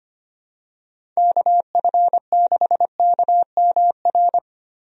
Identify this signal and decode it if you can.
Morse code, heard as a keyed tone, about 25 wpm, 700 Hz. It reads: KF6KMR
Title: USA Call Signs - 6 Characters - 25wpm